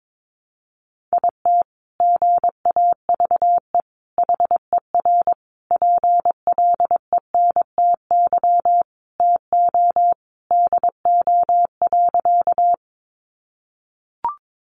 Morse code, heard as a keyed tone, about 22 wpm, 700 Hz. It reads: IT GAVE HER PLENTY TO DO.
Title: Sentences from Top 1300 Words 22wpm